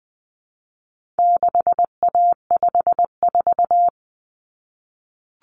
Morse code, heard as a keyed tone, about 20 wpm, 700 Hz. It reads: 6A54
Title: Common Cut Numbers - 4-Digit 20wpm